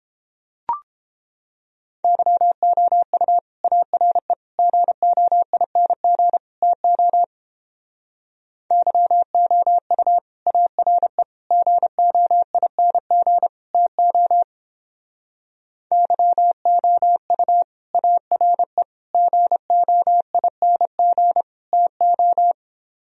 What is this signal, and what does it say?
Morse code, heard as a keyed tone, about 29 wpm, 700 Hz. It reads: YOU ARE GOING TO YOU ARE GOING TO YOU ARE GOING TO